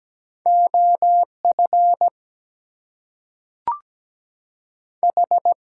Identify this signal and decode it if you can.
Morse code, heard as a keyed tone, about 17 wpm, 700 Hz. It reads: OF H
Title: Two Letter Words - 17wpm